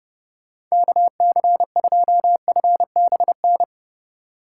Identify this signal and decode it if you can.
Morse code, heard as a keyed tone, about 39 wpm, 700 Hz. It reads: KC2FBN